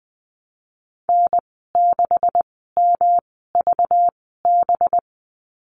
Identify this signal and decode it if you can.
Morse code, heard as a keyed tone, about 20 wpm, spaced as for 14 wpm, 700 Hz. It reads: N6MVB
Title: 193 Lesson 31g - Letter v - Call Signs 20wpm